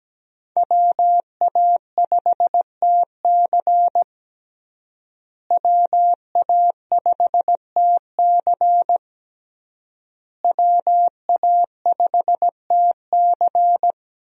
Morse code, heard as a keyed tone, about 17 wpm, 700 Hz. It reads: WA5TC WA5TC WA5TC
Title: USA Call Signs - 5 Characters - Repeated 3x - 17wpm